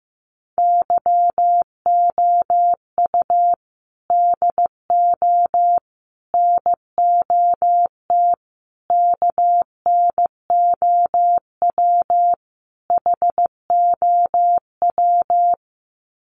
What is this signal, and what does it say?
Morse code, heard as a keyed tone, about 15 wpm, 700 Hz. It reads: YOU DO NOT KNOW HOW
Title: Sets of 5 Words Encore 15wpm